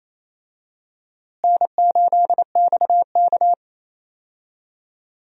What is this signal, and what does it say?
Morse code, heard as a keyed tone, about 27 wpm, 700 Hz. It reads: N8XK
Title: USA Call Signs - 4 Characters - Repeated 2x - 28wpm